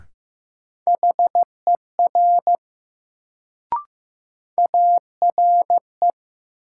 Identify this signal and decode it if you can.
Morse code, heard as a keyed tone, about 15 wpm, 700 Hz. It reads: HER ARE
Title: Sentences from Top 100 Words - New Word Drill - 15wpm